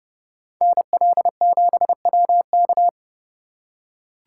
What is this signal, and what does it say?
Morse code, heard as a keyed tone, about 30 wpm, 700 Hz. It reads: NL7WK